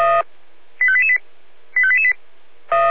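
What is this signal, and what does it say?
Five-tone selective call (ZVEI), heard as quick runs of five tones, tones of about 70 ms. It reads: 85888 85888
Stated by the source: Five-tone ZVEI call 85350 received by SDR.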